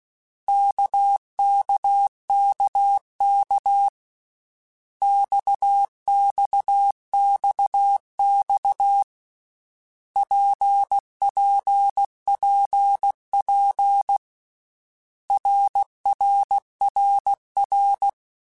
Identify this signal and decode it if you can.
Morse code, heard as a keyed tone, about 16 wpm, 800 Hz. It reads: KKKK XXXX PPPP RRRR
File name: Basic Letter Groups By7Groups / RKPX / RKPX@13